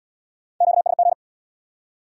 Morse code, heard as a keyed tone, about 75 wpm, 700 Hz. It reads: QSG